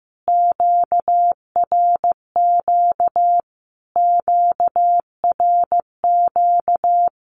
Morse code, heard as a keyed tone, about 15 wpm, 700 Hz. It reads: QRQ QRQ
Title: Q-Codes Repeated 3x 15wpm